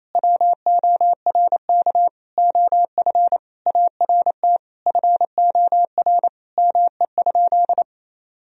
Morse code, heard as a keyed tone, about 28 wpm, 700 Hz. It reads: WORK OF ART FOR ME?
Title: Sentences from Top 1000 Words - Review Part 3 - 28wpm